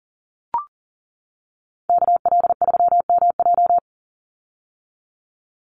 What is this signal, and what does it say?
Morse code, heard as a keyed tone, about 40 wpm, 700 Hz. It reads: KL3MJ
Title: USA Call Signs 40wpm